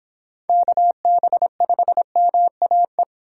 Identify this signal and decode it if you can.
Morse code, heard as a keyed tone, about 26 wpm, 700 Hz. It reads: KB5MAE